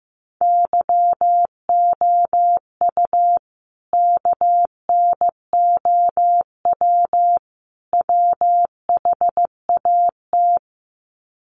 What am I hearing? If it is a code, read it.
Morse code, heard as a keyed tone, about 15 wpm, 700 Hz. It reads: YOU KNOW WHAT
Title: Sets of 3 Words 15wpm